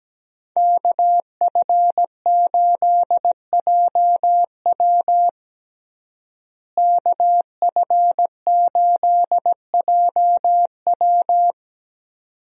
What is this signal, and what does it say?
Morse code, heard as a keyed tone, about 17 wpm, 700 Hz. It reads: KF8JW KF8JW
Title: USA Call Signs - 5 Characters - Repeated 2x - 17wpm